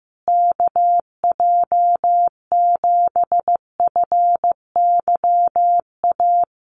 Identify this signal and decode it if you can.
Morse code, heard as a keyed tone, about 15 wpm, 700 Hz. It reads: KJ7FYA
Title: USA Call Signs Repeated 2x 15wpm